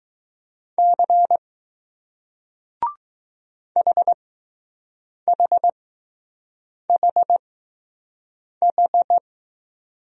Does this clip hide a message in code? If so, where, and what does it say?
Yes: Morse code from 0.8 to 9.2 s, heard as a keyed tone, about 20 wpm, spaced as for 6 wpm, 700 Hz. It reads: C HHHH